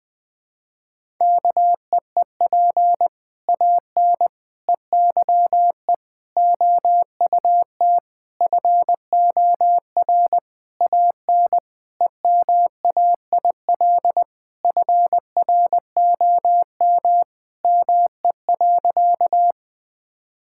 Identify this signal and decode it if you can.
Morse code, heard as a keyed tone, about 20 wpm, 700 Hz. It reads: KEEP AN EYE OUT FOR AN EMAIL FROM ME.
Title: Sentences with 7-Word Phrases - Spoken Beginning - 20wpm